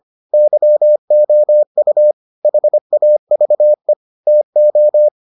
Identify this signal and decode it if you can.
Morse code, heard as a keyed tone, about 25 wpm, 600 Hz. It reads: YOU HAVE TO